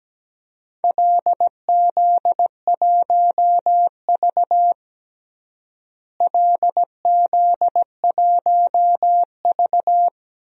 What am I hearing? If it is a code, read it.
Morse code, heard as a keyed tone, about 17 wpm, 700 Hz. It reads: LZ1V LZ1V
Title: DXCC-100 Call Signs - 4 Characters - Repeated 2x - 17wpm